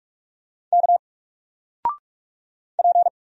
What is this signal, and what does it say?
Morse code, heard as a keyed tone, about 44 wpm, 700 Hz. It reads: K P
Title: Single Character (Letter:Number:Punctuation:Prosign) 45wpm